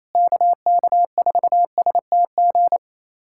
Morse code, heard as a keyed tone, about 28 wpm, 700 Hz. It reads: KK4STG
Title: USA Callsign DE Callsign 28wpm